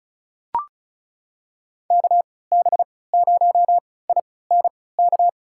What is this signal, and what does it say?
Morse code, heard as a keyed tone, about 34 wpm, spaced as for 20 wpm, 700 Hz. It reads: KB0INK
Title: USA Call Signs - 3x Character Spacing - 35wpm